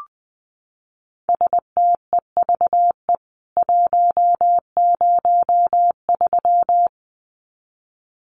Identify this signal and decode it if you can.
Morse code, heard as a keyed tone, about 20 wpm, 700 Hz. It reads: STEVE 103
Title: CWT Exchange 20wpm